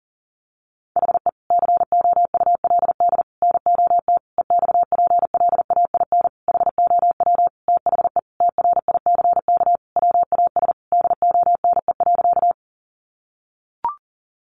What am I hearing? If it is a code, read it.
Morse code, heard as a keyed tone, about 37 wpm, 700 Hz. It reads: HE COULD NOT EXPLAIN HOW THE TRICK WAS DONE.
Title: Sentences from Top 1500 Words 40wpm